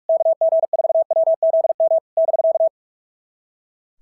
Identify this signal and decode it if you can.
Morse code, heard as a keyed tone, about 44 wpm, 650 Hz. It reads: KG4WZM <BK>
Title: SOTA QSO Practice 45wpm